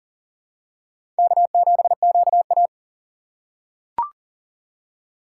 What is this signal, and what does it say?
Morse code, heard as a keyed tone, about 38 wpm, 700 Hz. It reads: K7QA
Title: USA Call Signs - 4 Characters - 6x Character Spacing - 40wpm